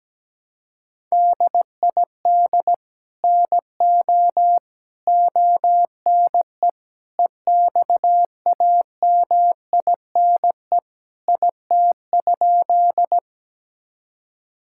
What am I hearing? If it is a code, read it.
Morse code, heard as a keyed tone, about 17 wpm, 700 Hz. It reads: DID NO ONE EXAMINE IT?
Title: Sentences from Top 1600 Words 17wpm